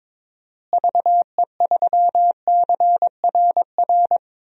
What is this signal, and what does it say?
Morse code, heard as a keyed tone, about 22 wpm, 700 Hz. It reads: VE3CRR
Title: DXCC-100 Call Signs - 6 Characters - Repeated 3x - 22wpm